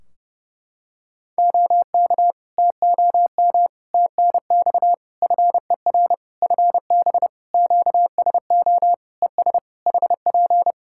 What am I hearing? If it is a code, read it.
Morse code, heard as a keyed tone, about 31 wpm, 700 Hz. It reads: OK TOM TNX FER FB QSO ES HP
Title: Practice QSO - Exchange 4 - Ending - 30wpm